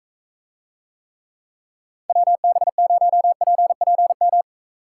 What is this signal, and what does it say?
Morse code, heard as a keyed tone, about 40 wpm, 700 Hz. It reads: WB0PPM